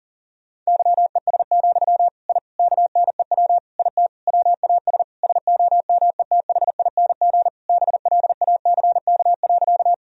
Morse code, heard as a keyed tone, about 40 wpm, 700 Hz. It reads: YES, I KNEW IT WAS SOMETHING BLACK.
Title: Sentences from Top 1000 Words - Review Part 2 - 40wpm